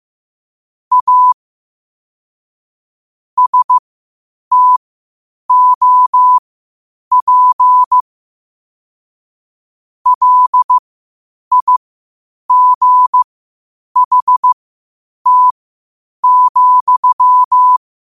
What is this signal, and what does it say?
Morse code, heard as a keyed tone, about 15 wpm, spaced as for 8 wpm, 1000 Hz. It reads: A STOP LIGHT,
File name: qotdfiles / 8wpm